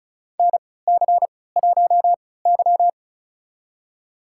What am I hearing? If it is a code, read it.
Morse code, heard as a keyed tone, about 36 wpm, spaced as for 20 wpm, 700 Hz. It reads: NC1Y
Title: USA Call Signs - 4 Characters - 3x Character Spacing - 35wpm